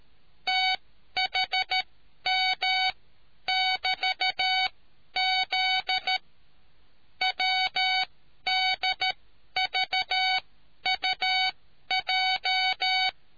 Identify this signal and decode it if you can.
Morse code, heard as a keyed tone, about 13 wpm, spaced as for 10 wpm, 750 Hz. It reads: THM<BT>Z WDVUJ